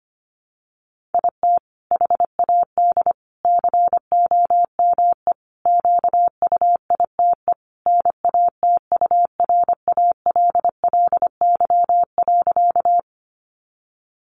Morse code, heard as a keyed tone, about 25 wpm, 700 Hz. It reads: IT HAD COME QUITE NATURALLY.
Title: Sentences from Top 1300 Words 25wpm